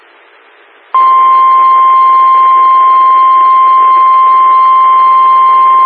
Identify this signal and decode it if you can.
Emergency Alert System (SAME) tones (the 1050 Hz attention tone). It is heard from 0.9 to 5.3 s.
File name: WX9MCS 147255 MHz SKYWARN Repeater